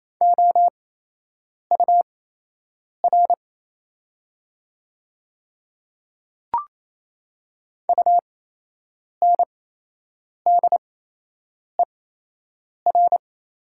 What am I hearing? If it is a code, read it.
Morse code, heard as a keyed tone, about 28 wpm, spaced as for 8 wpm, 700 Hz. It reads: OUR UNDER
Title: 8x Character Spacing - Top 200 Words - 28wpm